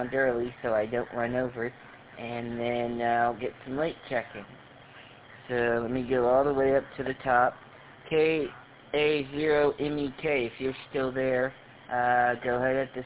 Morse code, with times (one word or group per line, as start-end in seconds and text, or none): none